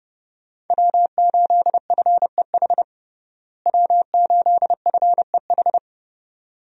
Morse code, heard as a keyed tone, about 30 wpm, 700 Hz. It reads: W8FEH W8FEH
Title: USA Call Signs - 5 Characters - Repeated 2x - 30wpm